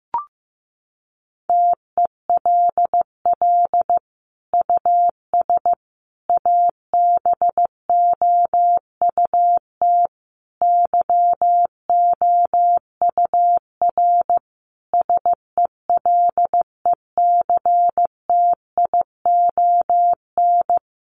Morse code, heard as a keyed tone, about 15 wpm, 700 Hz. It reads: TELL US ABOUT YOUR SELECTION
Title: Sentences from Top 1400 Words 15wpm